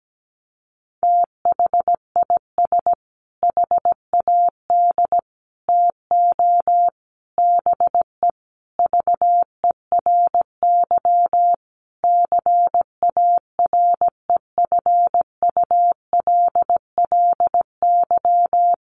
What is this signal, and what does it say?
Morse code, heard as a keyed tone, about 17 wpm, 700 Hz. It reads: THIS HAD TO BE VERY CAREFULLY
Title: Sentences from Top 1300 Words 17wpm